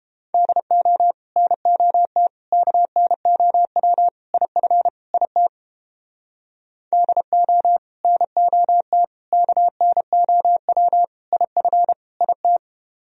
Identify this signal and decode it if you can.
Morse code, heard as a keyed tone, about 29 wpm, 700 Hz. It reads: DO NOT KNOW IF IT DO NOT KNOW IF IT